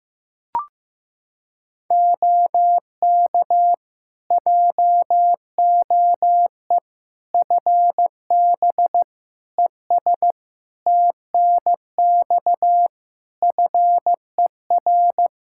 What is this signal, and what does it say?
Morse code, heard as a keyed tone, about 15 wpm, 700 Hz. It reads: OK JOE FB ES TNX FER
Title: K1USN SST QSO Practice 15wpm